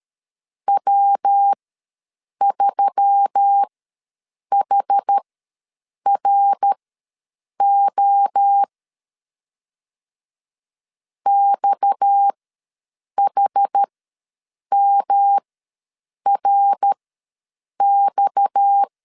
Morse code, heard as a keyed tone, about 13 wpm, spaced as for 7 wpm, 800 Hz. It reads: W3HRO XHMRX
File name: LEZIONI / 6 PKX0 / 6 PKX ws 12wpm 3106